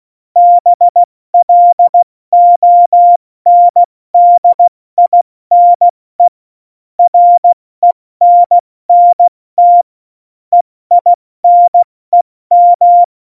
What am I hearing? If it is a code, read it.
Morse code, heard as a keyed tone, about 16 wpm, 700 Hz. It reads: BLONDINE RENNT EINEM
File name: Morsekurs / Witze / Witz 11 Juli 17, Tempo 70 F80